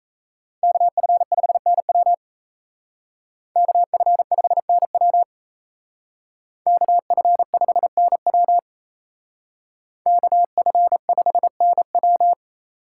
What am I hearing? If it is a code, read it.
Morse code, heard as a keyed tone, about 34 wpm, 700 Hz. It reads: KF5NW KF5NW KF5NW KF5NW
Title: Speed Racing - USA 2x2 Call Signs 28wpm